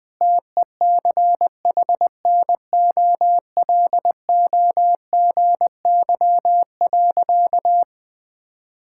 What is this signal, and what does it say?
Morse code, heard as a keyed tone, about 21 wpm, 700 Hz. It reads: TECHNOLOGY.
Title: Sentences from Top 1000 Words - Review Part 5 - 20wpm